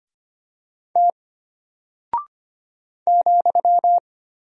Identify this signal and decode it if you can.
Morse code, heard as a keyed tone, about 25 wpm, 700 Hz. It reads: T ,
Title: Single Character (Letter:Number:Punctuation:Prosign) Alt 25wpm